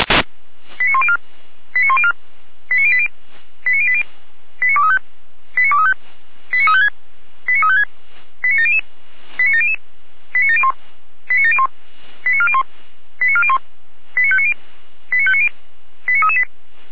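Five-tone selective call (ZVEI), heard as quick runs of five tones, tones of about 70 ms. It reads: 89104 89104 89080 89080 89235 89235 89356 89356 89700 89700 89701 89701 89401 89401 89590 89590 89338
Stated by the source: ZVEI five-tone selective call received by SDR.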